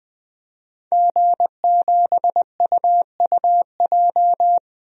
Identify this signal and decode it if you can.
Morse code, heard as a keyed tone, about 20 wpm, 700 Hz. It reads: G7UUJ
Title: DXCC-100 Call Signs - 5 Characters - Repeated 2x - 20wpm